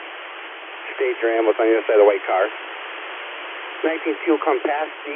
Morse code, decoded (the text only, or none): none